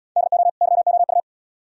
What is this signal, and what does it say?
Morse code, heard as a keyed tone, about 78 wpm, 700 Hz. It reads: DO YOU